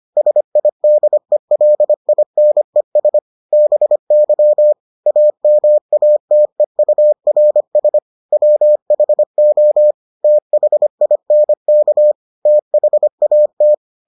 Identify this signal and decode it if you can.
Morse code, heard as a keyed tone, about 25 wpm, 600 Hz. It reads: SIDELINES BY AMATEURS WHO THINK THAT